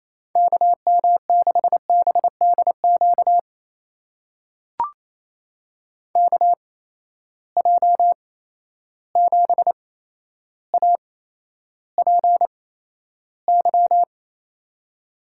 Morse code, heard as a keyed tone, about 28 wpm, spaced as for 12 wpm, 700 Hz. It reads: KM6BDQ KJ7APY
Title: USA Call Signs - 6 Characters - 8x Character Spacing - 28wpm